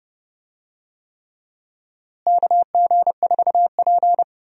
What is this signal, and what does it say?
Morse code, heard as a keyed tone, about 30 wpm, 700 Hz. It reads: KG4P
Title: USA Call Signs - 4 Characters - Repeated 3x - 30wpm